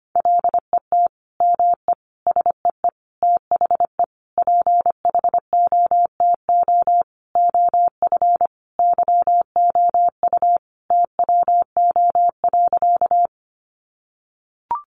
Morse code, heard as a keyed tone, about 25 wpm, 700 Hz. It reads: LET ME SEE THE PHOTO OF YOU TWO.
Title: Sentences from Top 1200 Words 25wpm